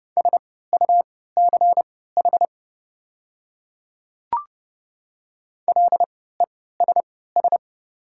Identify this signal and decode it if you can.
Morse code, heard as a keyed tone, about 30 wpm, spaced as for 17 wpm, 690 Hz. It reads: SUCH LESS
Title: 3x Character Spacing - Top 500 Words - 30wpm